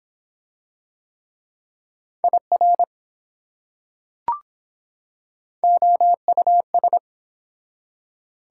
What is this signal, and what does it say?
Morse code, heard as a keyed tone, about 24 wpm, 700 Hz. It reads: IR OUS